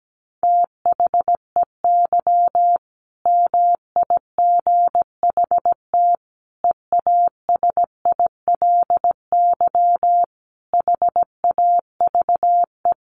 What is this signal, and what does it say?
Morse code, heard as a keyed tone, about 17 wpm, 700 Hz. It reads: THEY MIGHT EASILY HAVE
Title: Sentences from Top 1100 Words 17wpm